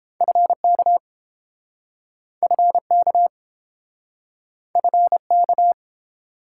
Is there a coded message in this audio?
Yes: Morse code from 0.2 to 5.7 s, heard as a keyed tone, about 28 wpm, 700 Hz. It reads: FK FK FK